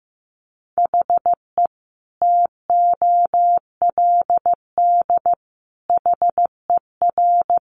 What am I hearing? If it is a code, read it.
Morse code, heard as a keyed tone, about 15 wpm, 700 Hz. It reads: HE TOLD HER
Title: Sentences from Top 1500 Words 15wpm